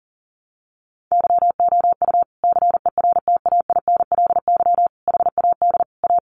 Morse code, heard as a keyed tone, about 40 wpm, 700 Hz. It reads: YOU CERTAINLY HAD A
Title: Sentences from Top 1200 Words 40wpm